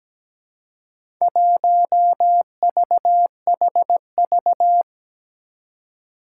Morse code, heard as a keyed tone, about 17 wpm, 700 Hz. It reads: 1VHV
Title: Four Challenging Letter-Number Combinations 17wpm